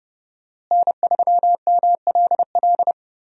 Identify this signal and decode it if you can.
Morse code, heard as a keyed tone, about 32 wpm, 700 Hz. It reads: N3MLL
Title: USA Call Signs - 5 Characters - Repeated 3x - 30wpm